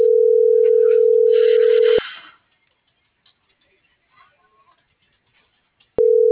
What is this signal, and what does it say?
Ringback tone (North America), 440 + 480 Hz.